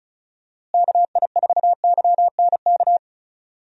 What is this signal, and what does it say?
Morse code, heard as a keyed tone, about 34 wpm, 700 Hz. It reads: KI4YNK
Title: USA Call Signs - 6 Characters - Repeated 3x - 35wpm